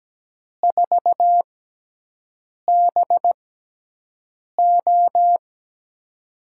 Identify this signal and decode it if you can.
Morse code, heard as a keyed tone, about 17 wpm, spaced as for 6 wpm, 700 Hz. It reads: 4BO
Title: USA Call Signs - 5 Characters - 6x Character Spacing - 17wpm